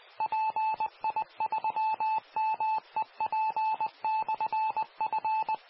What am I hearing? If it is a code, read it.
Morse code, heard as a keyed tone, about 20 wpm, 870 Hz. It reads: PI3MEP/F